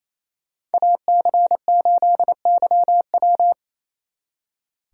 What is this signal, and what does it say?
Morse code, heard as a keyed tone, about 34 wpm, 700 Hz. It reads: AC8YW
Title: Speed Racing - USA Call Signs 28wpm